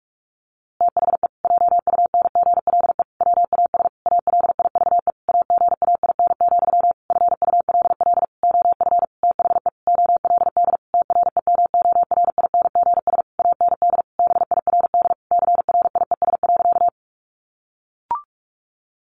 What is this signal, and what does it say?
Morse code, heard as a keyed tone, about 44 wpm, 700 Hz. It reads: THE JUNGLE WAS ALIVE AGAIN, FULL OF THE OLD TREMORINGS AND BIRD CRIES.
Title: A Sound Of Thunder 45wpm